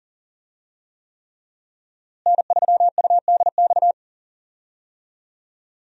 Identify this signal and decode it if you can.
Morse code, heard as a keyed tone, about 38 wpm, 700 Hz. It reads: N3UDX
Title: USA Call Signs - 5 Characters - Repeated 3x - 40wpm